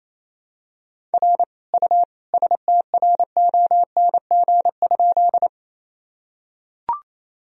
Morse code, heard as a keyed tone, about 28 wpm, 700 Hz. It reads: R U STRONG?
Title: Sentences from Top 500 Words and Abbreviations 28wpm